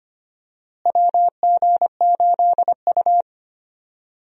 Morse code, heard as a keyed tone, about 25 wpm, 700 Hz. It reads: WG8U
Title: USA Call Signs - 4 Characters - Repeated 2x - 25wpm